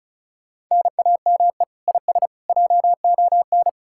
Morse code, heard as a keyed tone, about 48 wpm, spaced as for 42 wpm, 700 Hz. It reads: NAME IS JON
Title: Speed-Racing Common QSO Phrases Encore - 35wpm